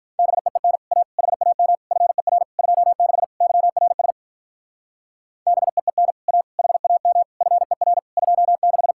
Morse code, heard as a keyed tone, about 44 wpm, 700 Hz. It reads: BEEN A HAM FER 26 YRS BEEN A HAM FER 26